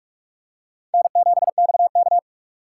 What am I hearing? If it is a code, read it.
Morse code, heard as a keyed tone, about 44 wpm, 700 Hz. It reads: N7XK
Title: USA Call Signs - 4 Characters - 6x Character Spacing - 45wpm